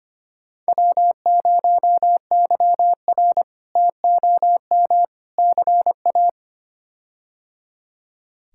Morse code, heard as a keyed tone, about 25 wpm, 700 Hz. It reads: W0YR TOM CA
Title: K1USN SST QSO Practice - 25wpm